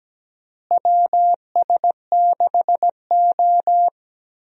Morse code, heard as a keyed tone, about 17 wpm, 700 Hz. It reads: WS6O
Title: USA Call Signs - 4 Characters - Repeated 3x - 17wpm